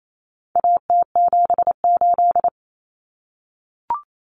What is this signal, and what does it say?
Morse code, heard as a keyed tone, about 28 wpm, 700 Hz. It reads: AT78
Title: Common Cut Numbers - 4-Digit 28wpm